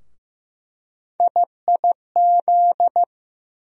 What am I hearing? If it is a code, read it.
Morse code, heard as a keyed tone, about 15 wpm, 700 Hz. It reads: IIZ
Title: USA Call Sign Postfixes - 3 Letters 15wpm